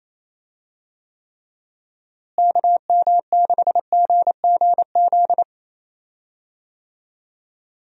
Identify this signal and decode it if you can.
Morse code, heard as a keyed tone, about 28 wpm, 700 Hz. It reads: KM6GGZ
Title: USA Call Signs - 6 Characters - Repeated 2x - 28wpm